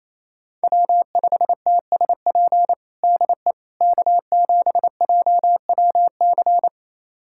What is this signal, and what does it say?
Morse code, heard as a keyed tone, about 28 wpm, 690 Hz. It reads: W5TSP DE K7JWC
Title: USA Callsign DE Callsign - 5 Characters - 28wpm